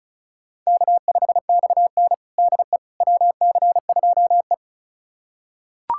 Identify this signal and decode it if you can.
Morse code, heard as a keyed tone, about 36 wpm, 690 Hz. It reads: K5XN DE WC2E
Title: USA Callsign DE Callsign - 4 Characters - 35wpm